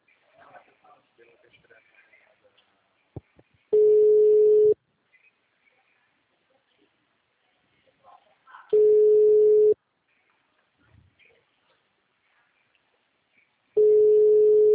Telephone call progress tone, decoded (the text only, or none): ringback tone